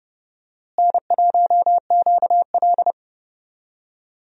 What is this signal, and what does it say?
Morse code, heard as a keyed tone, about 29 wpm, 700 Hz. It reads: N1QL
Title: USA Call Signs - 4 Characters - Repeated 2x - 30wpm